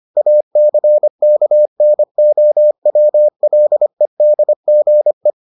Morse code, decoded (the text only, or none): ACKNOWLEDGE